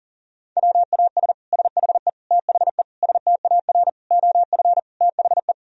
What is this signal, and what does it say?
Morse code, heard as a keyed tone, about 40 wpm, 700 Hz. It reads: WAS SHE THE STAR OF THE
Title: Sentences from Top 1000 Words - Review Part 3 - 40wpm